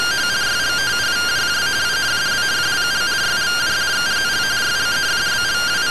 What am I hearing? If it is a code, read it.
Two-tone teleprinter signal, about 170 Hz shift.